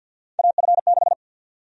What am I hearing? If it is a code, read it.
Morse code, heard as a keyed tone, about 52 wpm, 700 Hz. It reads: AF6
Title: USA Call Sign Prefixes - 50wpm